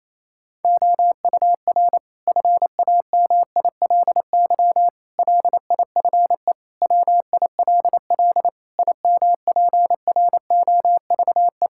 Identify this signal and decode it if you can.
Morse code, heard as a keyed tone, about 28 wpm, 700 Hz. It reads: OUR FAMILY LIFE WILL IMPROVE
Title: Sentences from Top 1000 Words - Review Part 5 - 28wpm